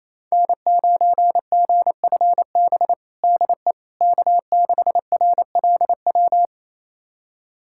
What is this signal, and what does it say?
Morse code, heard as a keyed tone, about 28 wpm, 700 Hz. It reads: N9GFB DE K6RLW
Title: USA Callsign DE Callsign - 5 Characters - 28wpm